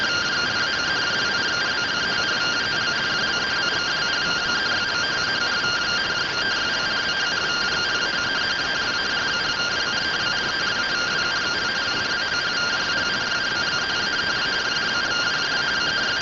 Two-tone teleprinter signal, about 170 Hz shift.